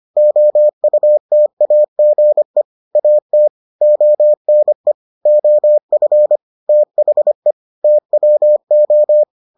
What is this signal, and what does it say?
Morse code, heard as a keyed tone, about 25 wpm, 600 Hz. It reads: OUTAGE AT ONE OF THE TWO